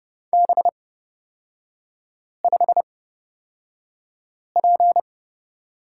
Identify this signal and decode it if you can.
Morse code, heard as a keyed tone, about 29 wpm, spaced as for 5 wpm, 700 Hz. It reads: B5P
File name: Lesson 36e - BT - Three Characters 30wpm